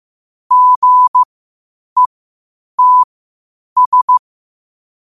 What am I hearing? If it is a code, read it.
Morse code, heard as a keyed tone, about 15 wpm, spaced as for 8 wpm, 1000 Hz. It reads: GETS
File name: qotdfiles / 8wpm